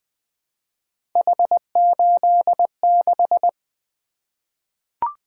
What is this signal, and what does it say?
Morse code, heard as a keyed tone, about 20 wpm, 700 Hz. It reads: H86
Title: Three Letter-Number Combinations - 20wpm